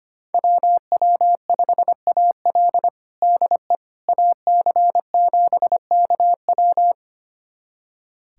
Morse code, heard as a keyed tone, about 25 wpm, 700 Hz. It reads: WW5AL DE AC7KW
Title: USA Callsign DE Callsign - 5 Characters - 25wpm